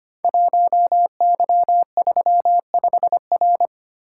Morse code, heard as a keyed tone, about 25 wpm, 700 Hz. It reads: 1Y35R
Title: Code Groups - 5 Letter:Numbers - 25wpm